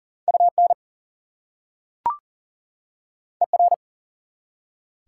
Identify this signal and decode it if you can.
Morse code, heard as a keyed tone, about 40 wpm, 700 Hz. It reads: UN ER
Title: Prefixes and Suffixes 40wpm